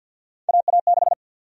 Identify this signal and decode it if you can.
Morse code, heard as a keyed tone, about 52 wpm, 700 Hz. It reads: AA6
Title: USA Call Sign Prefixes - Repeated 2x - 50wpm